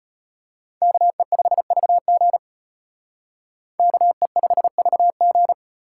Morse code, heard as a keyed tone, about 32 wpm, 700 Hz. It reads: KE5VG KE5VG